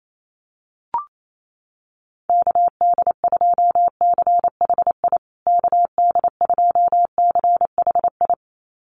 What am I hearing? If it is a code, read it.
Morse code, heard as a keyed tone, about 28 wpm, 700 Hz. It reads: KD2CHI KD2CHI
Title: USA Call Signs Repeated 2x 28wpm